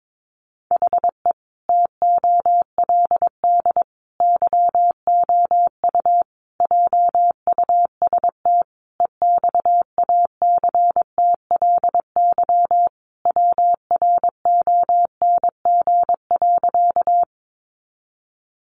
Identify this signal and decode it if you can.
Morse code, heard as a keyed tone, about 22 wpm, 700 Hz. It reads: HE TOLD YOU JUST EXACTLY WRONG.
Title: Sentences from Top 900 Words 22wpm